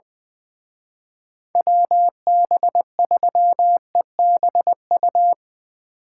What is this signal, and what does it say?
Morse code, heard as a keyed tone, about 22 wpm, 700 Hz. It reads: WB3EBU
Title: Speed Racing - USA 2x3 Call Signs 20wpm